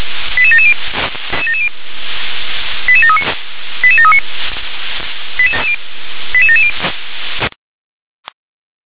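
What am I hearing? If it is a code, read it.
Five-tone selective call (ZVEI), heard as quick runs of five tones, tones of about 70 ms. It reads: 88770 88739 88739 88770
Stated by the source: Five-tone ZVEI selective call 88738 received by SDR.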